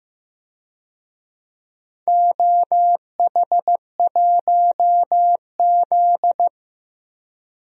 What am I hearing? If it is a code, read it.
Morse code, heard as a keyed tone, about 15 wpm, 700 Hz. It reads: OH1Z
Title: DXCC-100 Call Signs - 4 Characters - 15wpm